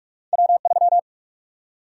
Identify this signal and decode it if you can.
Morse code, heard as a keyed tone, about 46 wpm, 700 Hz. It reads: W3
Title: USA Call Sign Prefixes - 45wpm